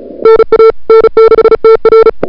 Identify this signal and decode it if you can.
Morse code, heard as a keyed tone, about 38 wpm, 440 Hz. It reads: NA N6TR
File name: n6tr / Fab2007CWSprintHour1